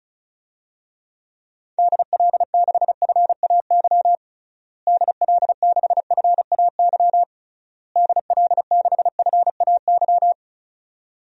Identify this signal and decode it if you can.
Morse code, heard as a keyed tone, about 34 wpm, 700 Hz. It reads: DL6FAY DL6FAY DL6FAY
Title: DXCC-100 Call Signs - 6 Characters - Repeated 3x - 35wpm